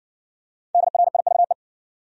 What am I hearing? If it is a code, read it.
Morse code, heard as a keyed tone, about 60 wpm, 700 Hz. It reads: DRIVE